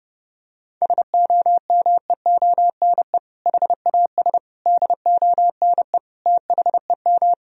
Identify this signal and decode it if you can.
Morse code, heard as a keyed tone, about 30 wpm, 700 Hz. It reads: SOMEONE HAS DONE THEM
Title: Sentences from Top 1000 Words - Review Part 4 - 30wpm